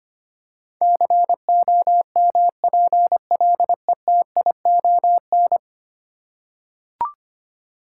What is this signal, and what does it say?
Morse code, heard as a keyed tone, about 25 wpm, 690 Hz. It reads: COMPLETION
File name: 10 Letter Words - First 3 Letters Given - 25wpm